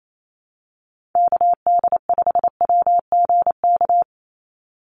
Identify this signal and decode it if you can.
Morse code, heard as a keyed tone, about 28 wpm, 700 Hz. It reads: KD5WGK
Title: USA Call Signs 28wpm